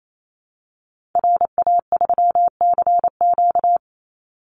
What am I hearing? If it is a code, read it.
Morse code, heard as a keyed tone, about 28 wpm, 700 Hz. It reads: RA3CQ
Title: Call Signs Repeated 3x - 28wpm